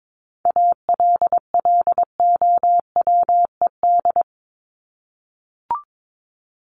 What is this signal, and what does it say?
Morse code, heard as a keyed tone, about 22 wpm, 700 Hz. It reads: ALLOWED
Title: ED Suffix - Top 100 Words with Base Words 22wpm